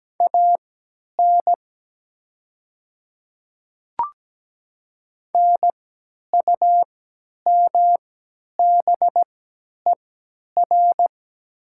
Morse code, heard as a keyed tone, about 17 wpm, spaced as for 10 wpm, 700 Hz. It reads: AN NUMBER
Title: 3x Character Spacing - Top 100 Words - 17wpm